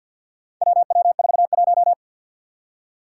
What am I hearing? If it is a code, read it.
Morse code, heard as a keyed tone, about 52 wpm, 700 Hz. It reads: WW41
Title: Four Challenging Letter-Number Combinations 50wpm